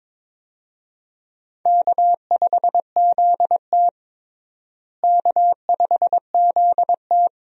Morse code, heard as a keyed tone, about 22 wpm, 700 Hz. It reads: K5ZT K5ZT
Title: USA Call Signs - 4 Characters - Repeated 2x - 22wpm